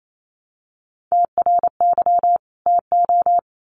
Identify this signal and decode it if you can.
Morse code, heard as a keyed tone, about 28 wpm, 700 Hz. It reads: TRY TO
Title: Sets of 2 Words 28wpm